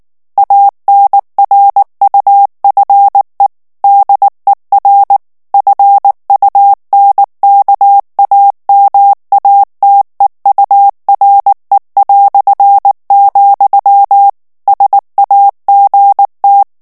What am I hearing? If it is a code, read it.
Morse code, heard as a keyed tone, about 19 wpm, 800 Hz. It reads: ANRUFE DER FUNKAMATEURE", SAGT
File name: morse mp3 / ALLEMAND texte clair 2 vitesse 18 mots minute